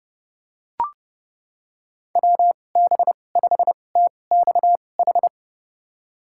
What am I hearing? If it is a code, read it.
Morse code, heard as a keyed tone, about 30 wpm, spaced as for 22 wpm, 700 Hz. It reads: WB5TXH
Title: USA Call Signs - 2x Character Spacing - 30wpm